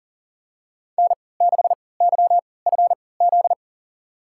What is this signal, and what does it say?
Morse code, heard as a keyed tone, about 40 wpm, spaced as for 23 wpm, 700 Hz. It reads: N6YFZ
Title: USA Call Signs - 5 Characters - 3x Character Spacing - 40wpm